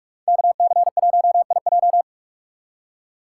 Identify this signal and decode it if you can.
Morse code, heard as a keyed tone, about 44 wpm, 700 Hz. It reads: KC1IJ